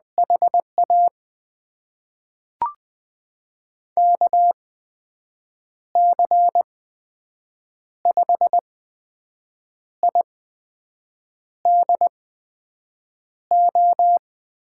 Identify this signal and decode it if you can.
Morse code, heard as a keyed tone, about 20 wpm, spaced as for 6 wpm, 700 Hz. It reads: HA KC5IDO
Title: USA Call Signs - 6 Characters - 8x Character Spacing - 20wpm